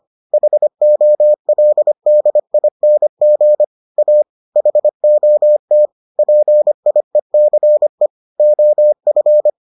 Morse code, heard as a keyed tone, about 25 wpm, 600 Hz. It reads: HOLDING A HOT PIECE OF